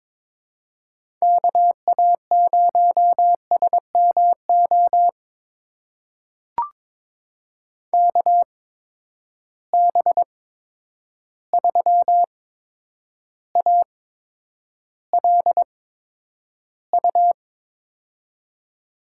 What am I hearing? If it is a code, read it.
Morse code, heard as a keyed tone, about 22 wpm, spaced as for 9 wpm, 700 Hz. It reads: KA0SMO KB3ALU
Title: USA Call Signs - 6 Characters - 8x Character Spacing - 22wpm